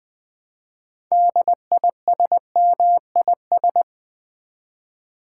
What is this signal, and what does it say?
Morse code, heard as a keyed tone, about 20 wpm, 700 Hz. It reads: DISMIS
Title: Build A Word - 3 Letters At A Time - 9-Letter Words - 20wpm